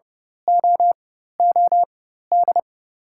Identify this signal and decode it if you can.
Morse code, heard as a keyed tone, about 30 wpm, spaced as for 14 wpm, 700 Hz. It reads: OOD
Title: 4x Character Spacing - Top 500 Words - 30wpm